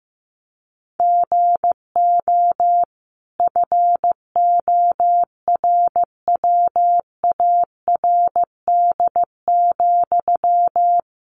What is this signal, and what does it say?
Morse code, heard as a keyed tone, about 15 wpm, 700 Hz. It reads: GO FORWARD,
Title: Sentences from Top 1300 Words 15wpm